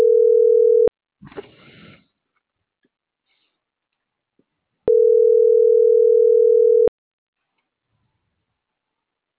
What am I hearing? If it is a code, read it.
Ringback tone (North America), 440 + 480 Hz.